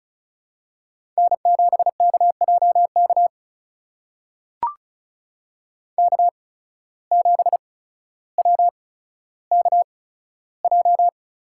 Morse code, heard as a keyed tone, about 36 wpm, spaced as for 15 wpm, 700 Hz. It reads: N7KJK K7WKJ
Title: USA Call Signs - 5 Characters - 8x Character Spacing - 35wpm